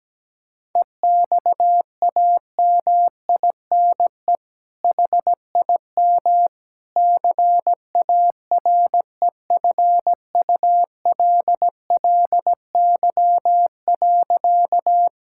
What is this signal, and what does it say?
Morse code, heard as a keyed tone, about 17 wpm, 700 Hz. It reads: EXAMINE HIM CAREFULLY.
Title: Sentences from Top 1600 Words 17wpm